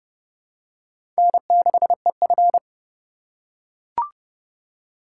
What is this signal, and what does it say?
Morse code, heard as a keyed tone, about 30 wpm, 700 Hz. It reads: N6EF
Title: USA Call Signs - 4 Characters - 4x Character Spacing - 30wpm